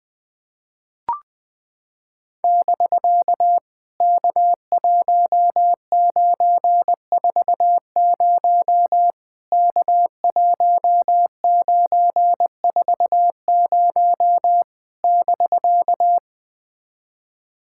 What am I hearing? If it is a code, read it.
Morse code, heard as a keyed tone, about 20 wpm, 700 Hz. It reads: <BK> K1940 K1940 <BK>
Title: POTA QSO Practice - 20wpm